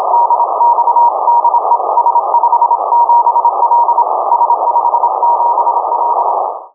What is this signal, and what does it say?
Morse code, heard as a keyed tone, about 26 wpm, 950 Hz, with noise. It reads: DX DE S57L S57L
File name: uploader / 50098 khz cw